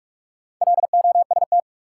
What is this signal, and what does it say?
Morse code, heard as a keyed tone, about 46 wpm, 700 Hz. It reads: LOST
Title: Sentences from Top 1400 Words - New Word Drill - 45wpm